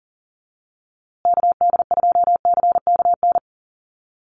Morse code, heard as a keyed tone, about 40 wpm, 700 Hz. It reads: KD2CKN
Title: USA Call Signs Repeated 2x 40wpm